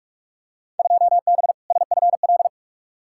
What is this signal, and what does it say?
Morse code, heard as a keyed tone, about 43 wpm, 700 Hz. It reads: 2B SFL
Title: Field Day Exchanges - 45wpm